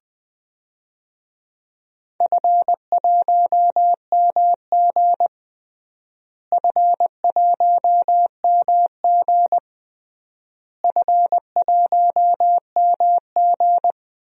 Morse code, heard as a keyed tone, about 20 wpm, 700 Hz. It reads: F1MG F1MG F1MG
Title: DXCC-100 Call Signs - 4 Characters - Repeated 3x - 20wpm